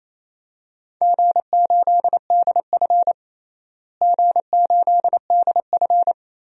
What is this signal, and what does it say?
Morse code, heard as a keyed tone, about 28 wpm, 700 Hz. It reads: G8DF G8DF
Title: DXCC-100 Call Signs - 4 Characters - Repeated 3x - 28wpm